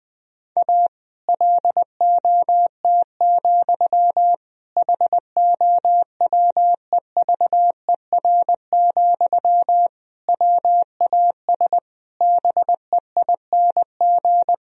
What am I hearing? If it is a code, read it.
Morse code, heard as a keyed tone, about 20 wpm, 700 Hz. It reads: A LOT, HOWEVER, WAS BEING
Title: Sentences from Top 1000 Words - Review Part 3 - 20wpm